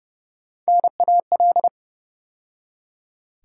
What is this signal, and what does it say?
Morse code, heard as a keyed tone, about 30 wpm, 700 Hz. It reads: NAL
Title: USA Call Sign Postfixes - 3 Letters 30wpm